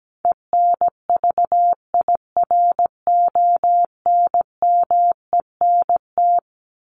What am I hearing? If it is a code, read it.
Morse code, heard as a keyed tone, about 17 wpm, 700 Hz. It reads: ENVIRONMENT
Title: Top 501-750 Words 17wpm